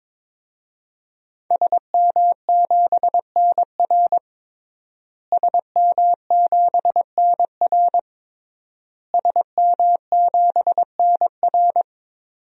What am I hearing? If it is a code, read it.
Morse code, heard as a keyed tone, about 22 wpm, 700 Hz. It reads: SM7NR SM7NR SM7NR
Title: DXCC-100 Call Signs - 5 Characters - Repeated 3x - 22wpm